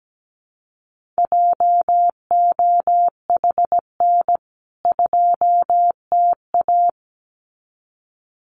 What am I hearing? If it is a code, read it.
Morse code, heard as a keyed tone, about 17 wpm, 700 Hz. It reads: JOHN 2TA
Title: CWT Exchange 17wpm